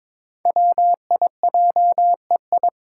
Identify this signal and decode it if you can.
Morse code, heard as a keyed tone, about 22 wpm, 700 Hz. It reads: WIJEI
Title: Five Challenging Letter-Number Combinations 22wpm